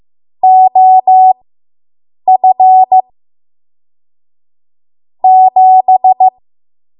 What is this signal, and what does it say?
Morse code, heard as a keyed tone, about 15 wpm, spaced as for 7 wpm, 750 Hz. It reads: OF 7